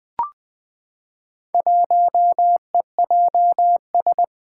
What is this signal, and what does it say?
Morse code, heard as a keyed tone, about 20 wpm, 700 Hz. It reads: 1EJS
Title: Four Challenging Letter-Number Combinations 20wpm